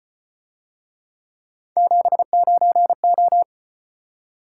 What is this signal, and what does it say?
Morse code, heard as a keyed tone, about 32 wpm, 700 Hz. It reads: 79O